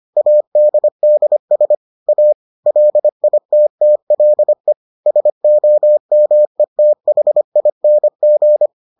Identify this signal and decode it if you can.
Morse code, heard as a keyed tone, about 25 wpm, 600 Hz. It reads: ADDS A LITTLE SOMETHING